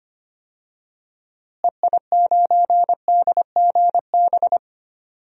Morse code, heard as a keyed tone, about 25 wpm, 700 Hz. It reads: EI9DGB
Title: DXCC-100 Call Signs - 6 Characters - Repeated 2x - 25wpm